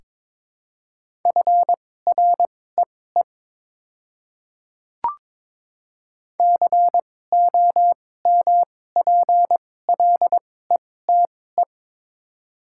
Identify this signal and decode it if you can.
Morse code, heard as a keyed tone, about 22 wpm, spaced as for 16 wpm, 700 Hz. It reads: FREE COMPLETE
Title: 2x Character Spacing - Top 500 Words - 22wpm